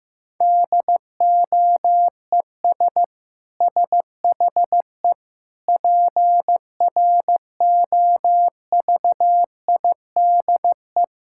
Morse code, heard as a keyed tone, about 15 wpm, 700 Hz. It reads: DOES SHE PROVIDE
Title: Sentences from Top 1000 Words - Review Part 6 - 15wpm